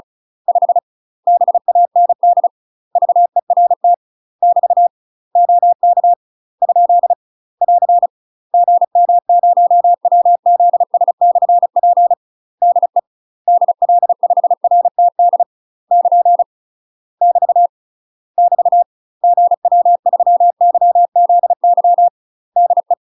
Morse code, heard as a keyed tone, about 36 wpm, 700 Hz. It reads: 5 BAND VERT <BT> OK ? <AR> GM0WZS/P DE DL5RTD <KN> <BT> <BT> GW3YZY DE
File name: UK MASTER QSOs TRAINING / 35WPM / UK MASTER QSOs 0041